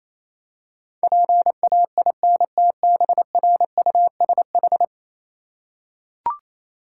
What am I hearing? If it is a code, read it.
Morse code, heard as a keyed tone, about 27 wpm, 700 Hz. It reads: PAINTBRUSH
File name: Easy Words - Speed-Racing with Hints - 22wpm